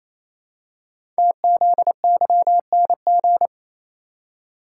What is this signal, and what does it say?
Morse code, heard as a keyed tone, about 28 wpm, 700 Hz. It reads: TZYNG
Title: Code Groups - 5 Letter:Numbers - 28wpm